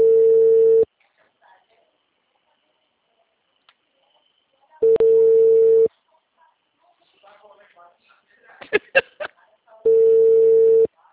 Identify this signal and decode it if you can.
Ringback tone (Europe, 425 Hz), with speech.